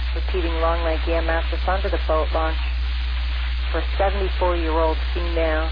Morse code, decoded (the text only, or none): none